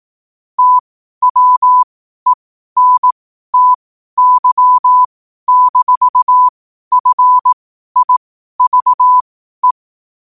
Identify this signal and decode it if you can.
Morse code, heard as a keyed tone, about 18 wpm, spaced as for 12 wpm, 1000 Hz. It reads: TWENTY-FIVE
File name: qotdfiles / 12wpm